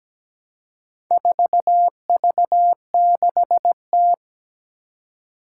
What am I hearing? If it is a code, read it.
Morse code, heard as a keyed tone, about 17 wpm, 700 Hz. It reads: 4V6T
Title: Four Challenging Letter-Number Combinations 17wpm